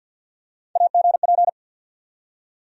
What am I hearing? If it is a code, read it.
Morse code, heard as a keyed tone, about 51 wpm, 700 Hz. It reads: AGP
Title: USA Call Sign Postfixes - 3 Letters 50wpm